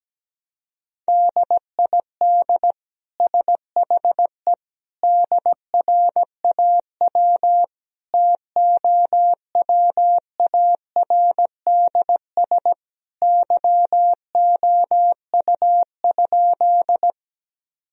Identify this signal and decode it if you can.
Morse code, heard as a keyed tone, about 17 wpm, 700 Hz. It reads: DID SHE DRAW TOWARDS YOU?
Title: Sentences from Top 1000 Words - Review Part 2 - 17wpm